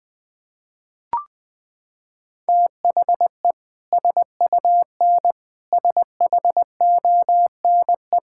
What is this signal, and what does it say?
Morse code, heard as a keyed tone, about 20 wpm, 700 Hz. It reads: THE SUN SHONE
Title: Sentences with 2-Word Phrases - 20wpm